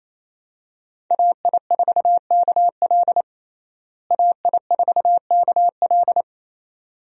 Morse code, heard as a keyed tone, about 27 wpm, 700 Hz. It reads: AI4KL AI4KL
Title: USA Call Signs - 5 Characters - Repeated 2x - 28wpm